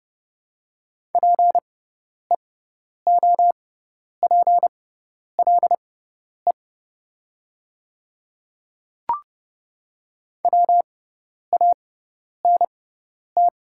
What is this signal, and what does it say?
Morse code, heard as a keyed tone, about 31 wpm, spaced as for 10 wpm, 700 Hz. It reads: PEOPLE WANT
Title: 6x Character Spacing - Top 200 Words - 30wpm